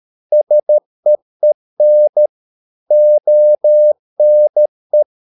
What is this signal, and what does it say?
Morse code, heard as a keyed tone, about 13 wpm, 600 Hz. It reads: SEEN ONE